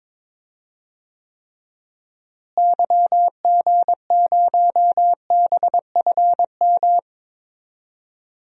Morse code, heard as a keyed tone, about 22 wpm, 700 Hz. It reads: YG0BFM
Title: DXCC-100 Call Signs - 6 Characters - Repeated 2x - 22wpm